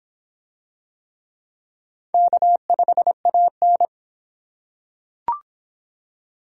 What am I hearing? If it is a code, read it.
Morse code, heard as a keyed tone, about 24 wpm, 700 Hz. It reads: K5AN